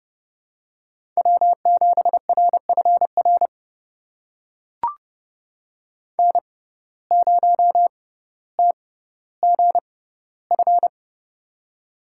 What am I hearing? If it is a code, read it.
Morse code, heard as a keyed tone, about 32 wpm, spaced as for 16 wpm, 700 Hz. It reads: W7RFR N0TGF
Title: USA Call Signs - 5 Characters - 6x Character Spacing - 30wpm